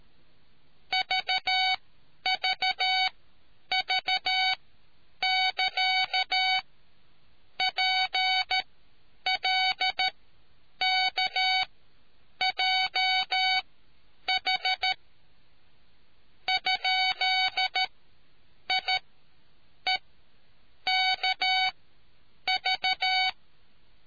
Morse code, heard as a keyed tone, about 13 wpm, spaced as for 8 wpm, 750 Hz. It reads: VVV <KA> PLKJH ?IEKV